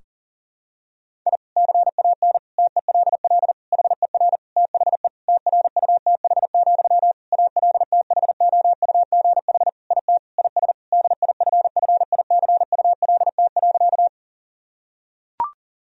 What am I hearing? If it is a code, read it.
Morse code, heard as a keyed tone, about 40 wpm, 700 Hz. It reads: I CAN TELL HER THE TRUTH, ALTHOUGH IT IS DIFFICULT.
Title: Sentences from Top 1000 Words - Review Part 3 - 40wpm